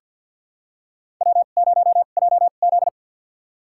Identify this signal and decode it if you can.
Morse code, heard as a keyed tone, about 47 wpm, spaced as for 35 wpm, 700 Hz. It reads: W0JZ
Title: USA Call Signs - 4 Characters - 2x Character Spacing - 50wpm